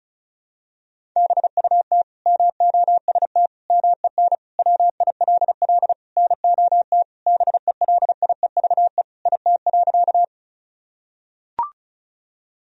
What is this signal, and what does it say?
Morse code, heard as a keyed tone, about 34 wpm, 700 Hz. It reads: BUT MOST MEN WILL NOT BELIEVE IT.
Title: Sentences from Top 1000 Words - Review Part 1 - 35wpm